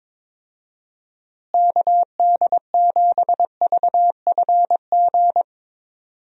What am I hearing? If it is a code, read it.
Morse code, heard as a keyed tone, about 22 wpm, 700 Hz. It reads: KD7VFG
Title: USA Call Signs - 6 Characters - Repeated 2x - 22wpm